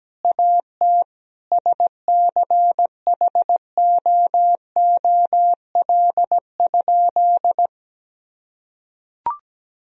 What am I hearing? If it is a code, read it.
Morse code, heard as a keyed tone, about 17 wpm, 700 Hz. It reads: AT SCHOOL?
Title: Sentences from Top 1000 Words - Review Part 4 - 17wpm